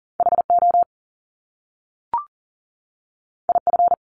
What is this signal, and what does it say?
Morse code, heard as a keyed tone, about 40 wpm, 700 Hz. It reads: HO IF
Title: Top Two Letter Combinations in Words 40wpm